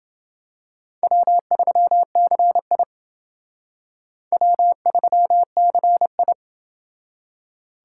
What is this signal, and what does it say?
Morse code, heard as a keyed tone, about 29 wpm, 700 Hz. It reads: W3CI W3CI